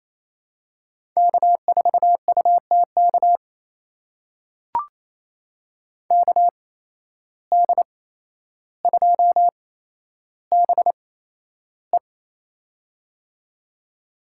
Morse code, heard as a keyed tone, about 28 wpm, spaced as for 12 wpm, 700 Hz. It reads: K4UTK KD2BE
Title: USA Call Signs - 5 Characters - 8x Character Spacing - 28wpm